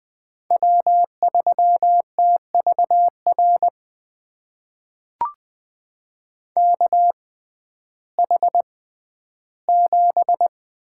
Morse code, heard as a keyed tone, about 20 wpm, spaced as for 12 wpm, 700 Hz. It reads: W3TVR KH7
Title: USA Call Signs - 5 Characters - 6x Character Spacing - 20wpm